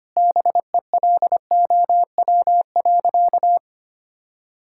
Morse code, heard as a keyed tone, about 25 wpm, 700 Hz. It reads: BELOW.
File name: practice / 25wpm / Sentences with 9-Word Phrases - 25wpm